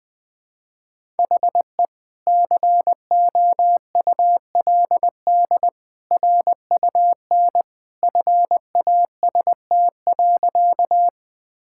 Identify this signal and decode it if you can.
Morse code, heard as a keyed tone, about 21 wpm, 700 Hz. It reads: HE COULD RUN FAST.
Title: Sentences with 2-Word Phrases - Spoken Beginning - 20wpm